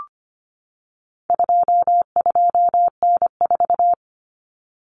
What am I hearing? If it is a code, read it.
Morse code, heard as a keyed tone, about 25 wpm, 700 Hz. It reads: 22N4
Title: Common Cut Numbers - 4-Digit 25wpm